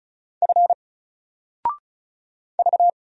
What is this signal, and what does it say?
Morse code, heard as a keyed tone, about 35 wpm, 700 Hz. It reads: F V
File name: Single-Letter-Number-35wpm